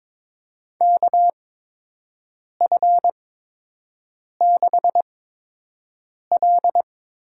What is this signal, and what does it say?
Morse code, heard as a keyed tone, about 22 wpm, spaced as for 6 wpm, 700 Hz. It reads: KF6L
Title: USA Call Signs - 5 Characters - 8x Character Spacing - 22wpm